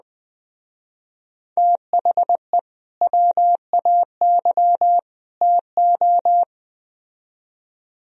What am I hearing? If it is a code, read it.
Morse code, heard as a keyed tone, about 20 wpm, 700 Hz. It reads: THE WAY TO